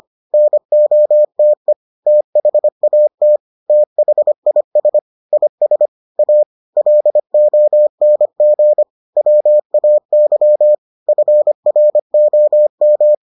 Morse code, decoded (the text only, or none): NOTE THAT THIS IS A LONG WAY FROM